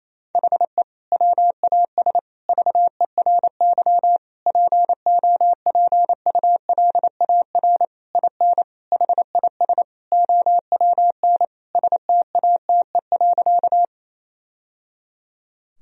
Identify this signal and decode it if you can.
Morse code, heard as a keyed tone, about 28 wpm, 700 Hz. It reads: HE WAS VERY POPULAR IN HIS OWN STATE.
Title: Sentences from Top 1000 Words - Review Part 3 - 28wpm